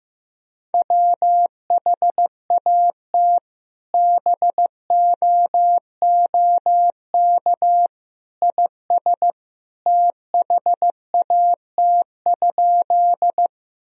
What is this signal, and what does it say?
Morse code, heard as a keyed tone, about 15 wpm, 700 Hz. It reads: WHAT BOOK IS THAT?
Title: Sentences from Top 1000 Words - Review Part 1 - 15wpm